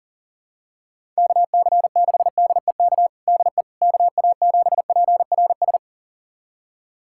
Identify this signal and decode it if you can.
Morse code, heard as a keyed tone, about 37 wpm, 700 Hz. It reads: KC6DEK DE KA7PRS
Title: USA Callsign DE Callsign - 6 Characters - 40wpm